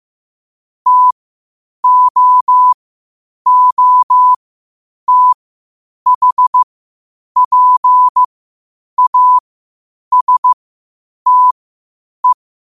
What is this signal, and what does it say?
Morse code, heard as a keyed tone, about 15 wpm, spaced as for 8 wpm, 1000 Hz. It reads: TOOTHPASTE
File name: qotdfiles / 8wpm